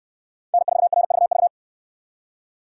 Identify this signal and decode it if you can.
Morse code, heard as a keyed tone, about 65 wpm, 700 Hz. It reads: N4RVV